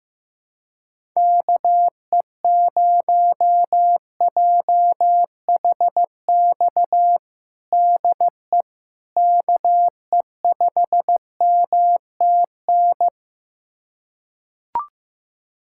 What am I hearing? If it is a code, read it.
Morse code, heard as a keyed tone, about 16 wpm, 700 Hz. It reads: KE0JHX DE KE5MTN
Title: USA Callsign DE Callsign - 6 Characters - 15wpm